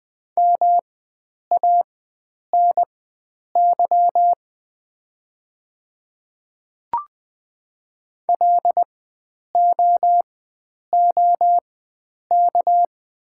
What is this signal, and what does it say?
Morse code, heard as a keyed tone, about 20 wpm, spaced as for 9 wpm, 690 Hz. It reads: MANY LOOK
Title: 4x Character Spacing - Top 100 Words - 20wpm